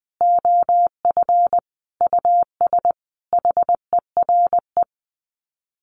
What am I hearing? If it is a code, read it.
Morse code, heard as a keyed tone, about 19 wpm, 700 Hz. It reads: OF US HERE
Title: Sets of 5 Words 20wpm